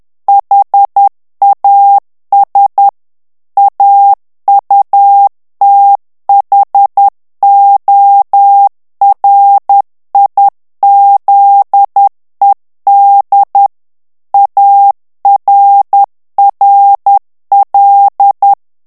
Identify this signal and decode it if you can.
Morse code, heard as a keyed tone, about 10 wpm, 800 Hz. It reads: HAS AUTHORIZED ARRL
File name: ANGLAIS texte clair 3 vitesse 10 mots minute